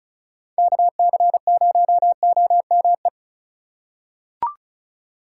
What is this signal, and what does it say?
Morse code, heard as a keyed tone, about 34 wpm, 700 Hz. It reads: KC0OME
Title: USA Call Signs - 6 Characters - 6x Character Spacing - 35wpm